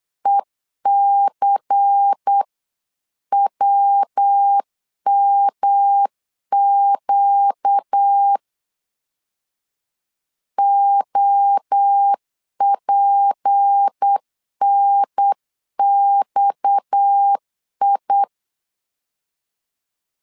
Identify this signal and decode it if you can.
Morse code, heard as a keyed tone, about 8 wpm, 800 Hz. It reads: ECWMQ OPNXI
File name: CW / DET / 1 6 1500C S C 8wpm 5219